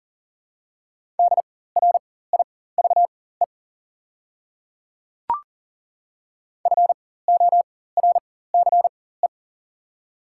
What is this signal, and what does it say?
Morse code, heard as a keyed tone, about 40 wpm, spaced as for 19 wpm, 700 Hz. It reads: DRIVE FORCE
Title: 4x Character Spacing - Top 500 Words - 40wpm